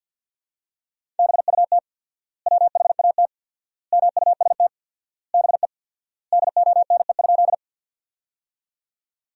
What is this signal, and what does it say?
Morse code, heard as a keyed tone, about 50 wpm, 700 Hz. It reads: BUT WHAT MUST BE DONE?
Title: 4x Word-Spacing - Sentences from Top 500 Words - 50wpm